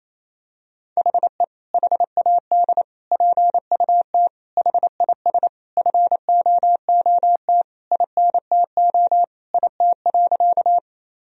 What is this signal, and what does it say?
Morse code, heard as a keyed tone, about 28 wpm, 700 Hz. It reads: HE HAD PUT HIS FOOT INTO IT.
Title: Sentences from Top 1000 Words - Review Part 3 - 28wpm